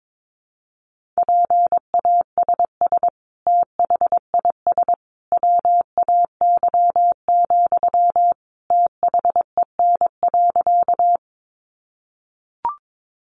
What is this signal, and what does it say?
Morse code, heard as a keyed tone, about 22 wpm, 700 Hz. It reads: PASS THIS WAY, THEN.
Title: Sentences from Top 400 Words 22wpm